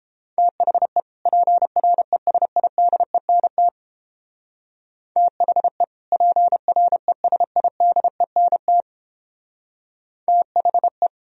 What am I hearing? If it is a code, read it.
Morse code, heard as a keyed tone, about 29 wpm, 700 Hz. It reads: THE PRESIDENT THE PRESIDENT THE